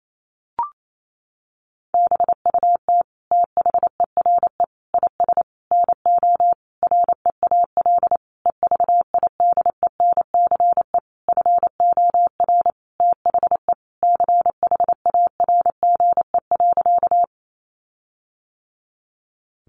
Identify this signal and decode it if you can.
Morse code, heard as a keyed tone, about 28 wpm, 700 Hz. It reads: BUT THERE IS NO REAL EVIDENCE FOR THE CHARGE.
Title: Sentences from Top 1300 Words 28wpm